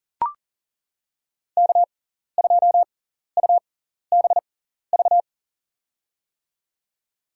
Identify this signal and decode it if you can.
Morse code, heard as a keyed tone, about 38 wpm, spaced as for 14 wpm, 700 Hz. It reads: K2UBV
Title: USA Call Signs - 5 Characters - 6x Character Spacing - 40wpm